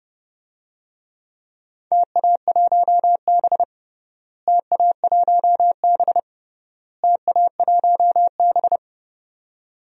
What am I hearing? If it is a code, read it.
Morse code, heard as a keyed tone, about 30 wpm, 700 Hz. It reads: TA1B TA1B TA1B
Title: DXCC-100 Call Signs - 4 Characters - Repeated 3x - 30wpm